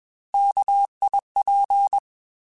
Morse code, heard as a keyed tone, about 21 wpm, 800 Hz. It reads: KIP
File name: Alphabetical / Alphabetical mp3s / Ks @20